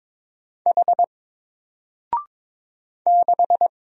Morse code, heard as a keyed tone, about 22 wpm, 700 Hz. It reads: H 6
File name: Single-Letter-Number-22wpm